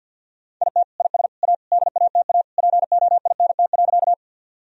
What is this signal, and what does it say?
Morse code, heard as a keyed tone, about 47 wpm, 700 Hz. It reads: IT IS A DATA POINT.
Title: Sentences from Top 1000 Words - Review Part 3 - 50wpm